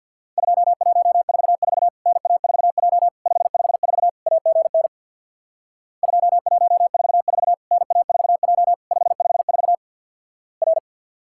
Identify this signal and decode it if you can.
Morse code, heard as a keyed tone, about 50 wpm, 700 Hz. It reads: 2144 NA4J 554 AGN 2144 NA4J 554 R
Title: Worked All Europe DX Contest - QTC Exchange with Repeats - 50wpm